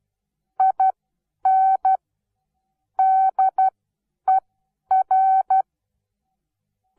Morse code, heard as a keyed tone, about 12 wpm, spaced as for 9 wpm, 760 Hz. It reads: IN DER
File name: Onlinelehrgang / morsen / DJ4UF Morsen17